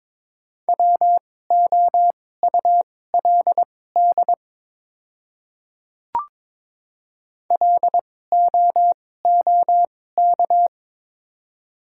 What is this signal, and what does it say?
Morse code, heard as a keyed tone, about 22 wpm, spaced as for 16 wpm, 700 Hz. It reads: WOULD LOOK
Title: 2x Character Spacing - Top 100 Words - 22wpm